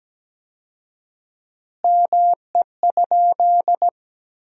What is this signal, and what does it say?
Morse code, heard as a keyed tone, about 17 wpm, 700 Hz. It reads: ME?
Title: 4x Word-Spacing - Sentences from Top 200 Words - 17wpm